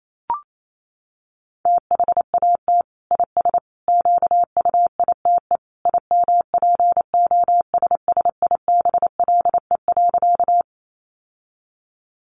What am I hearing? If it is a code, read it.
Morse code, heard as a keyed tone, about 28 wpm, 700 Hz. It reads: THAT IS QUITE IMPOSSIBLE.
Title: Sentences from Top 1100 Words 28wpm